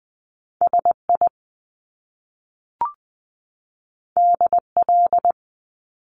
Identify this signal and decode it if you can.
Morse code, heard as a keyed tone, about 20 wpm, 700 Hz. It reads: S I D L
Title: Top Two Letter Combinations in Words 20wpm